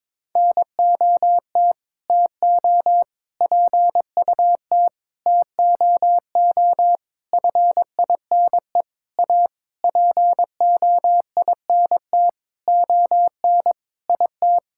Morse code, heard as a keyed tone, about 22 wpm, 700 Hz. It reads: NOT TO PUT TOO FINE A POINT ON IT
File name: Sentences with 9-Word Phrases - 22wpm